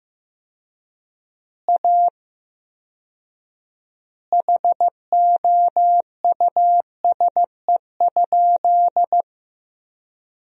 Morse code, heard as a keyed tone, about 16 wpm, 700 Hz. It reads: A HOUSE?
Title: 4x Word-Spacing - Sentences from Top 200 Words - 15wpm